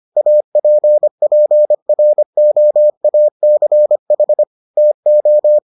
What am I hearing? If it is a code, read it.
Morse code, heard as a keyed tone, about 25 wpm, 600 Hz. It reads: APPROACH TO